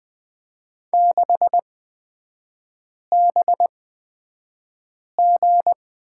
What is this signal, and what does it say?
Morse code, heard as a keyed tone, about 20 wpm, spaced as for 5 wpm, 700 Hz. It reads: 6BG
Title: 250 Lesson 40e - BK - Three Characters 20wpm